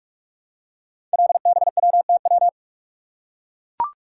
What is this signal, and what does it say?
Morse code, heard as a keyed tone, about 44 wpm, 700 Hz. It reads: LBWTW
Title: Code Groups - 5 Letter:Numbers - 45wpm